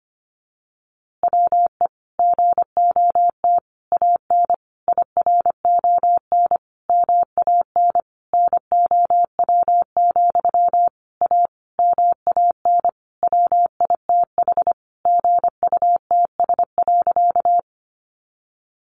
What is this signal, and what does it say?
Morse code, heard as a keyed tone, about 25 wpm, 700 Hz. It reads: WE GOT AN IRON MAN NOW, A MAN WITH GUTS.
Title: A Sound Of Thunder 25wpm